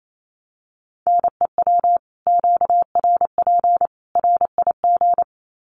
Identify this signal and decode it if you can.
Morse code, heard as a keyed tone, about 28 wpm, 700 Hz. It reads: NEW QRP RIG
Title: Common QSO Phrases 28wpm